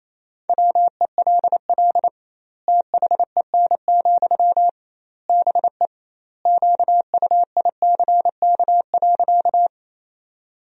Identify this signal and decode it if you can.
Morse code, heard as a keyed tone, about 28 wpm, 700 Hz. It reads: WELL THEN, BE QUICK.
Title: 2x Word-Spacing - Sentences from Top 500 Words - 28wpm